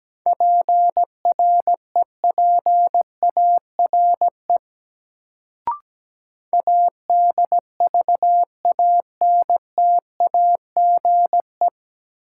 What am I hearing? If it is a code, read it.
Morse code, heard as a keyed tone, about 17 wpm, 700 Hz. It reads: PREPARE ADVANTAGE
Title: Sentences from Top 900 Words - New Word Drill - 17wpm